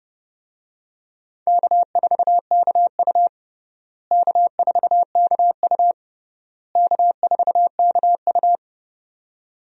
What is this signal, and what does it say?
Morse code, heard as a keyed tone, about 30 wpm, 700 Hz. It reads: K4KU K4KU K4KU
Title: USA Call Signs - 4 Characters - Repeated 3x - 30wpm